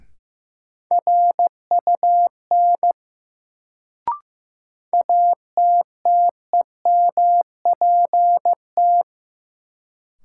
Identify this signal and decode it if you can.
Morse code, heard as a keyed tone, about 15 wpm, 700 Hz. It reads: RUN ATTEMPT
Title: Sets of 4 Words - New Word Drill - 15wpm